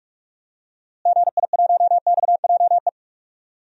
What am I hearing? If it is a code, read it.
Morse code, heard as a keyed tone, about 41 wpm, 700 Hz. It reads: GI1XJE
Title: DXCC-100 Call Signs - 6 Characters - Repeated 2x - 45wpm